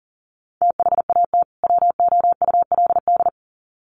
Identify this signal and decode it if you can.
Morse code, heard as a keyed tone, about 40 wpm, 700 Hz. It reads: THAT WOULD
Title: Sets of 2 Words 40wpm